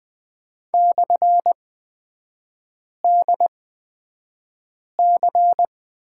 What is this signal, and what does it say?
Morse code, heard as a keyed tone, about 20 wpm, spaced as for 5 wpm, 700 Hz. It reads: /DC
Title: 229 Lesson 37e - Letter x - Three Characters 20wpm